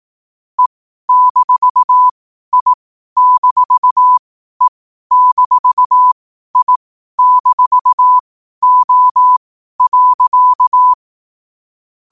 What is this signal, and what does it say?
Morse code, heard as a keyed tone, about 18 wpm, spaced as for 12 wpm, 1000 Hz. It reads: E-I-E-I-O.
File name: qotdfiles / 12wpm